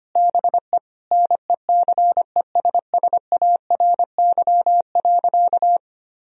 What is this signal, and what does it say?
Morse code, heard as a keyed tone, about 25 wpm, 700 Hz. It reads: BE NECESSARY.
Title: Sentences from Top 1000 Words - Review Part 3 - 25wpm